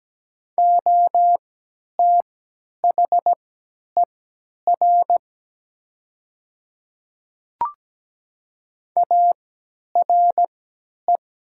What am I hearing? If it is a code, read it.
Morse code, heard as a keyed tone, about 17 wpm, spaced as for 10 wpm, 700 Hz. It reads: OTHER ARE
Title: 3x Character Spacing - Top 100 Words - 17wpm